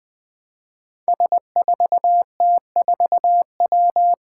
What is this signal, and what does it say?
Morse code, heard as a keyed tone, about 20 wpm, 700 Hz. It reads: S4T4W
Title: Five Challenging Letter-Number Combinations 20wpm